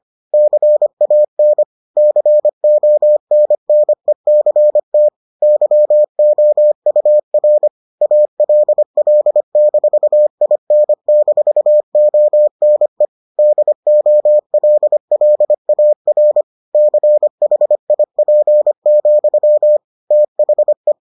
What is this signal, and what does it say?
Morse code, heard as a keyed tone, about 25 wpm, 600 Hz. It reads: CAN CONNECT YOUR ALL-IN-ONE DOLLAR CHIP, THE